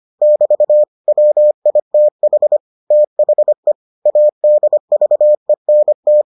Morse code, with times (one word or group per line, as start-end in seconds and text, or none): 0.2-0.8 <BT>
1.1-2.6 WITH
2.9-3.7 THE
4.1-6.2 ADVENT